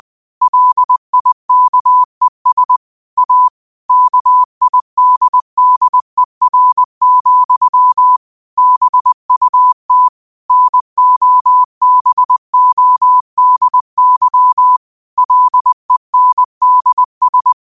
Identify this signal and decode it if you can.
Morse code, heard as a keyed tone, about 20 wpm, 1000 Hz. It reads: LIKES A KIDDER, BUT NOBODY LENDS
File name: qotdfiles / 20wpm